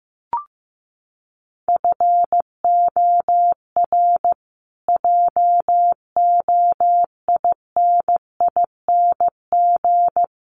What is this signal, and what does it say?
Morse code, heard as a keyed tone, about 15 wpm, 700 Hz. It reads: FOR JOINING
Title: Sets of 5 Words Encore 15wpm